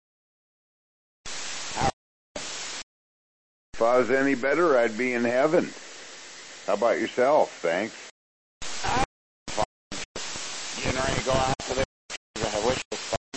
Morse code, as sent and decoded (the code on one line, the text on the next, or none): none